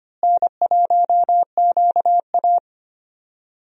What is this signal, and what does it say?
Morse code, heard as a keyed tone, about 25 wpm, 700 Hz. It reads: N1QA
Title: USA Call Signs - 4 Characters - Repeated 2x - 25wpm